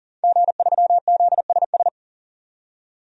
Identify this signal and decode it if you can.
Morse code, heard as a keyed tone, about 38 wpm, 700 Hz. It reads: G3ZSS
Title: DXCC-100 Call Signs - 5 Characters - 40wpm